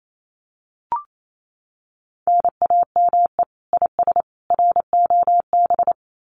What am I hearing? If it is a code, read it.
Morse code, heard as a keyed tone, about 28 wpm, 700 Hz. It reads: NAME IS ROB
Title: Common QSO Phrases 28wpm